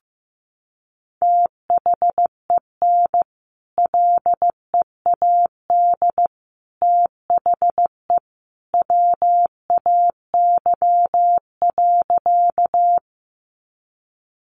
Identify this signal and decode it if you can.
Morse code, heard as a keyed tone, about 15 wpm, 700 Hz. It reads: THEN LEAD THE WAY.
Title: Sentences from Top 500 Words 15wpm